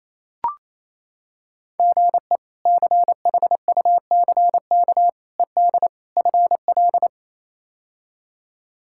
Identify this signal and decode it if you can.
Morse code, heard as a keyed tone, about 28 wpm, 700 Hz. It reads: GE CHUCK ED FL
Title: K1USN SST QSO Practice - 28wpm